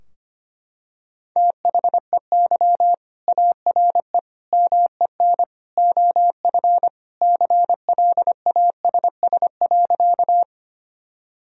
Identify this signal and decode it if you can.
Morse code, heard as a keyed tone, about 25 wpm, 700 Hz. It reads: THEY ARE MEN OF CLASS.
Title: Sentences from Top 1000 Words - Review Part 2 - 25wpm